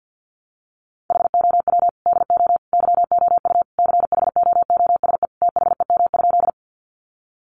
Morse code, heard as a keyed tone, about 50 wpm, 700 Hz. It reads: HOW DO YOU CHOOSE THEM?
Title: Sentences from Top 700 Words 50wpm